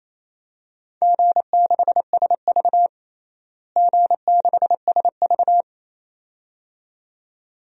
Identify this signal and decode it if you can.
Morse code, heard as a keyed tone, about 28 wpm, 700 Hz. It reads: G6SV G6SV
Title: DXCC-100 Call Signs - 4 Characters - Repeated 2x - 28wpm